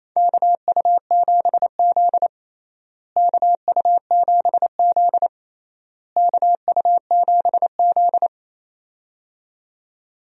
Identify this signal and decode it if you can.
Morse code, heard as a keyed tone, about 28 wpm, 700 Hz. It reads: KU7Z KU7Z KU7Z
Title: USA Call Signs - 4 Characters - Repeated 3x - 28wpm